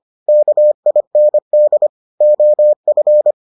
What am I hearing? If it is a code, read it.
Morse code, heard as a keyed tone, about 25 wpm, 600 Hz. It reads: KIND OF